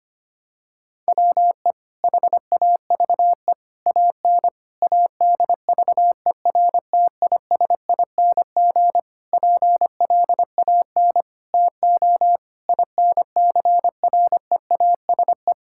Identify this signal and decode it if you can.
Morse code, heard as a keyed tone, about 25 wpm, 700 Hz. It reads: WE HAVE AN ADVERTISING PLAN TO INCREASE
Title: Sentences from Top 1000 Words - Review Part 6 - 25wpm